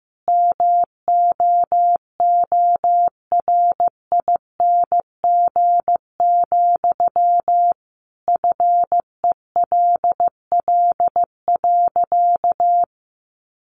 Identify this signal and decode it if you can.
Morse code, heard as a keyed tone, about 15 wpm, 700 Hz. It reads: MOORING, FELL.
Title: A Sound Of Thunder 15wpm